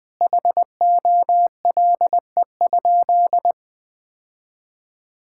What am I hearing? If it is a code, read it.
Morse code, heard as a keyed tone, about 20 wpm, 700 Hz. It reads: HOLE?
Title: Sentences from Top 1600 Words 20wpm